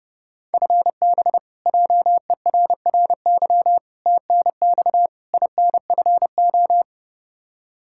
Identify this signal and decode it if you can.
Morse code, heard as a keyed tone, about 30 wpm, 700 Hz. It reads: FB JERRY TNX INFO
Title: K1USN SST QSO Practice 30wpm